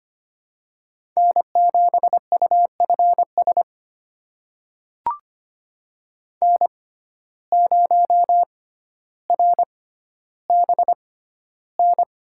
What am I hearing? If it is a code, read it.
Morse code, heard as a keyed tone, about 25 wpm, spaced as for 13 wpm, 700 Hz. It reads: N7UFS N0RBN
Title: USA Call Signs - 5 Characters - 6x Character Spacing - 25wpm